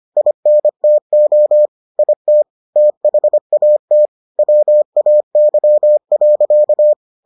Morse code, heard as a keyed tone, about 25 wpm, 600 Hz. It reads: INTO IT THAT WAY.